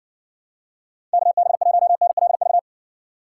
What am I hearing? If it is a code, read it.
Morse code, heard as a keyed tone, about 70 wpm, 700 Hz. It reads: KB1NLV